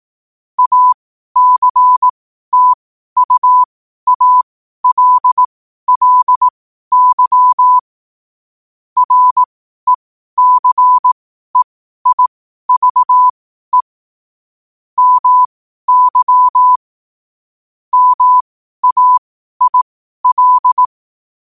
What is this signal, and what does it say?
Morse code, heard as a keyed tone, about 18 wpm, spaced as for 12 wpm, 1000 Hz. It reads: ACTUALLY RECEIVE MY MAIL
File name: qotdfiles / 12wpm